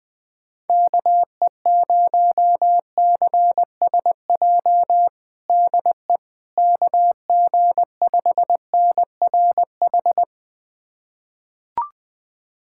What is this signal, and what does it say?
Morse code, heard as a keyed tone, about 19 wpm, 700 Hz. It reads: KE0CSJ DE KG5NRH
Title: USA Callsign DE Callsign - 6 Characters - 20wpm